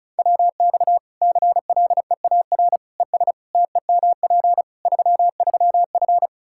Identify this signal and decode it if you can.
Morse code, heard as a keyed tone, about 36 wpm, 700 Hz. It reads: WX CLEAR ES TEMP 33F
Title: Practice QSO - Exchange 2 - Info - 35wpm